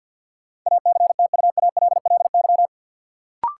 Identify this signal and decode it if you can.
Morse code, heard as a keyed tone, about 49 wpm, 700 Hz. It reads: ACTUALLY
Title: Sentences from Top 700 Words - New Word Drill - 50wpm